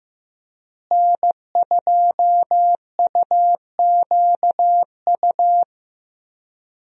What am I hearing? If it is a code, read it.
Morse code, heard as a keyed tone, about 15 wpm, 700 Hz. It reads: N2UQU
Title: USA Call Signs - 5 Characters - Repeated 2x - 15wpm